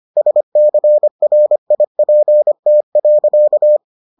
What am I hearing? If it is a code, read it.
Morse code, heard as a keyed tone, about 25 wpm, 600 Hz. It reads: SCRIPT.